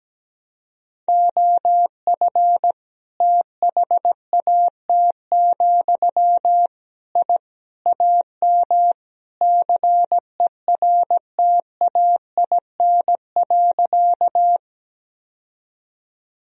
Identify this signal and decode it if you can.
Morse code, heard as a keyed tone, about 17 wpm, 700 Hz. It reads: OF THAT, I AM CERTAIN.
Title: Sentences from Top 1000 Words - Review Part 2 - 17wpm